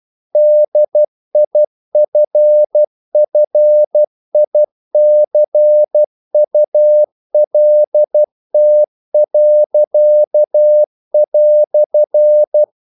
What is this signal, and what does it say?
Morse code, heard as a keyed tone, about 12 wpm, 600 Hz. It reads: DIFFICULT."
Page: English quotations in CW (telegraphy - morse)